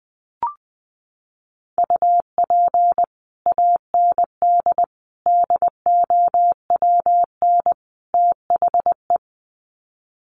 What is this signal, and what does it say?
Morse code, heard as a keyed tone, about 20 wpm, 700 Hz. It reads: UP AND DOWN THE
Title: Sets of 4 Words 20wpm